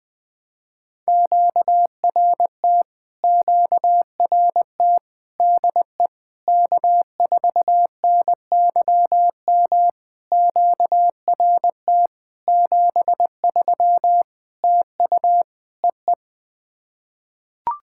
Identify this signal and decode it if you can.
Morse code, heard as a keyed tone, about 21 wpm, 700 Hz. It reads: QRT QRT DE K4NYM QRT 73 TU EE
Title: POTA QSO Practice - 20wpm